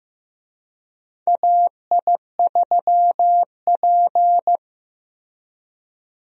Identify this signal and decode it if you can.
Morse code, heard as a keyed tone, about 15 wpm, 700 Hz. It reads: AI3P
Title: USA Call Signs - 4 Characters - Repeated 3x - 15wpm